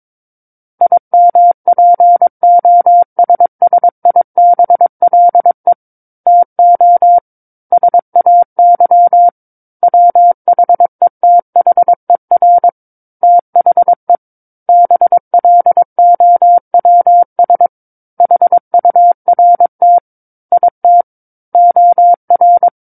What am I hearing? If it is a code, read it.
Morse code, heard as a keyed tone, about 22 wpm, 700 Hz. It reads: IMPOSSIBLE TO SAY WHETHER THE BLOWS HURT IT OR
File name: 06-Chapter-06 22 WPM